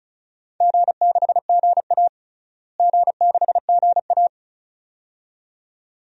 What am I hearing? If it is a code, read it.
Morse code, heard as a keyed tone, about 34 wpm, 700 Hz. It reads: G6GA G6GA
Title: DXCC-100 Call Signs - 4 Characters - Repeated 2x - 35wpm